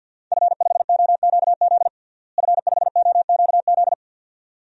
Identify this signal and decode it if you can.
Morse code, heard as a keyed tone, about 48 wpm, 700 Hz. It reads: F5OQZ F5OQZ
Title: DXCC-100 Call Signs - 5 Characters - Repeated 2x - 50wpm